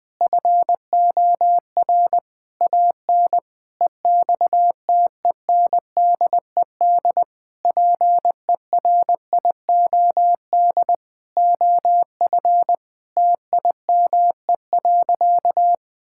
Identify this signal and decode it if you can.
Morse code, heard as a keyed tone, about 21 wpm, 700 Hz. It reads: FOR AN EXTENDED PERIOD OF TIME.
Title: Sentences with 6-Word Phrases - 20wpm